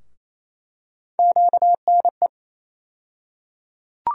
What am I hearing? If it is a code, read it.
Morse code, heard as a keyed tone, about 28 wpm, 700 Hz. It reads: QNE
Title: USA Call Sign Postfixes - 3 Letters 28wpm